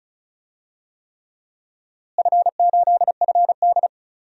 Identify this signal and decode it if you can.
Morse code, heard as a keyed tone, about 36 wpm, 700 Hz. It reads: F8FD
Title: DXCC-100 Call Signs - 4 Characters - Repeated 3x - 35wpm